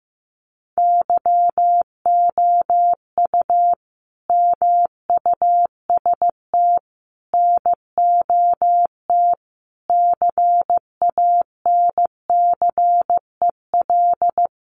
Morse code, heard as a keyed tone, about 15 wpm, 700 Hz. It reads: YOU MUST NOT CANCEL
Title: Sentences from Top 1200 Words 15wpm